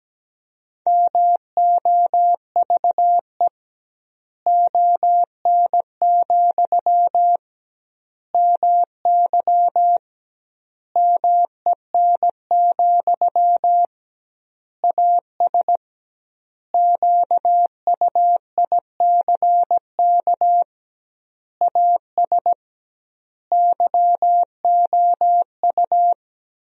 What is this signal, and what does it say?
Morse code, heard as a keyed tone, about 17 wpm, 700 Hz. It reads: MOVE ON, MY MEN, AS QUICK AS YOU
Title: 2x Word-Spacing - Sentences from Top 500 Words - 17wpm